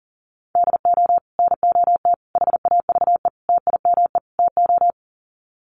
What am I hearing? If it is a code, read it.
Morse code, heard as a keyed tone, about 40 wpm, 700 Hz. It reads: DO NOT HAVE TIME TO
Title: Sets of 5 Words 40wpm